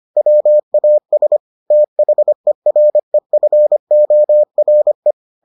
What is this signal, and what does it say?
Morse code, heard as a keyed tone, about 25 wpm, 600 Hz. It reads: WAS THEREFORE